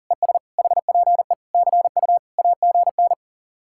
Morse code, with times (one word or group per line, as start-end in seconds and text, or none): 0.1-0.4 ES
0.6-1.3 HPE
1.5-2.2 CU
2.4-3.1 AGN